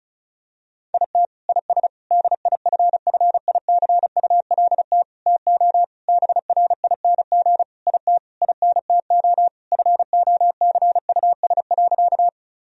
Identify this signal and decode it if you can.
Morse code, heard as a keyed tone, about 36 wpm, 700 Hz. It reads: IT IS DIFFICULT TO BRING IT INTO FOCUS.
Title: Sentences from Top 1000 Words - Review Part 4 - 35wpm